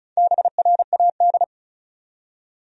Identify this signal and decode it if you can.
Morse code, heard as a keyed tone, about 36 wpm, 700 Hz. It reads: BRAD
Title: USA Common Names - Max 5 Letters - 35wpm